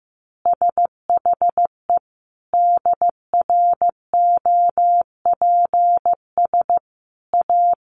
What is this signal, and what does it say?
Morse code, heard as a keyed tone, about 15 wpm, 700 Hz. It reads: SHE DROPS A
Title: Easy Phrases 15wpm